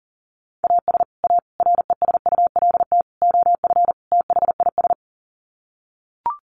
Morse code, heard as a keyed tone, about 40 wpm, 700 Hz. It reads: AS A RESULT OF THIS
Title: Sets of 5 Words 40wpm